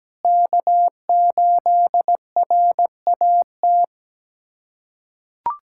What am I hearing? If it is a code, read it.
Morse code, heard as a keyed tone, about 17 wpm, 700 Hz. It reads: K8RAT
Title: USA Callsign DE Callsign 17wpm